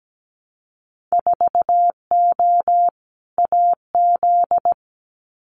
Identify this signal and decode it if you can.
Morse code, heard as a keyed tone, about 17 wpm, 700 Hz. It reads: 4O AZ
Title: Winter Field Day Exchanges Encore 17wpm